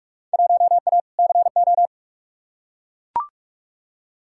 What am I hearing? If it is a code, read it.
Morse code, heard as a keyed tone, about 42 wpm, 700 Hz. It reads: 1A CO
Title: Field Day Exchanges - 45wpm